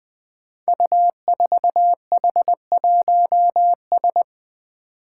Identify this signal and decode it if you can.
Morse code, heard as a keyed tone, about 20 wpm, 700 Hz. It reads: U4H1S
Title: Five Challenging Letter-Number Combinations 20wpm